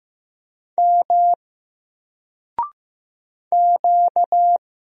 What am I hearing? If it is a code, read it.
Morse code, heard as a keyed tone, about 15 wpm, 700 Hz. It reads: M Q
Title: Single Character (Letter:Number:Punctuation:Prosign) 15wpm